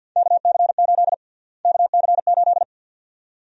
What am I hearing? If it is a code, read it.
Morse code, heard as a keyed tone, about 52 wpm, 700 Hz. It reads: KC8 KC8
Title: USA Call Sign Prefixes - Repeated 2x - 50wpm